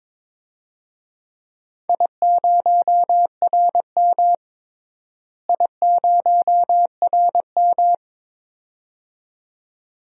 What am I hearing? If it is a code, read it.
Morse code, heard as a keyed tone, about 22 wpm, 700 Hz. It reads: I0RM I0RM
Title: DXCC-100 Call Signs - 4 Characters - Repeated 2x - 22wpm